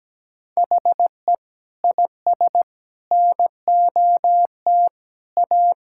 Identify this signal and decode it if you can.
Morse code, heard as a keyed tone, about 17 wpm, 700 Hz. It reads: HE IS NOT A
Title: Sentences from Top 1600 Words 17wpm